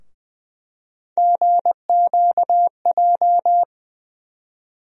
Morse code, heard as a keyed tone, about 20 wpm, 700 Hz. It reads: GQJ
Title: USA Call Sign Postfixes - 3 Letters 20wpm